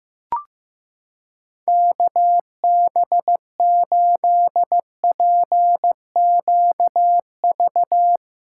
Morse code, heard as a keyed tone, about 15 wpm, 700 Hz. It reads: KB8PQV
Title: USA Callsign DE Callsign - 6 Characters - 15wpm